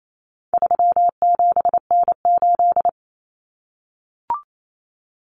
Morse code, heard as a keyed tone, about 28 wpm, 700 Hz. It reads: 37N8
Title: Common Cut Numbers - 4-Digit 28wpm